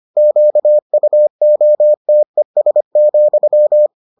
Morse code, heard as a keyed tone, about 25 wpm, 600 Hz. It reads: QUOTES,